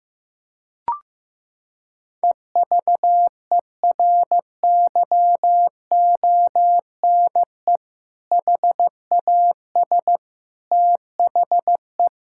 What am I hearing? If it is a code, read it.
Morse code, heard as a keyed tone, about 15 wpm, 700 Hz. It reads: EVERYONE HAS THE
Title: Sentences from Top 1000 Words - Review Part 5 - 15wpm